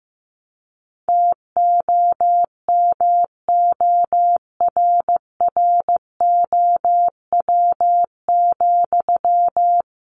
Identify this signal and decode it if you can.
Morse code, heard as a keyed tone, about 15 wpm, 700 Hz. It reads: TOMORROW,
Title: Sentences from Top 1500 Words 15wpm